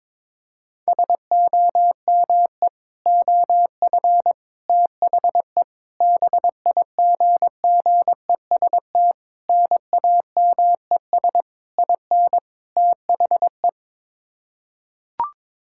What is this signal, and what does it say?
Morse code, heard as a keyed tone, about 22 wpm, 700 Hz. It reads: SOME OF THE BIGGEST NAMES IN THE
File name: practice / 22wpm / Sets of 7 Words - 22wpm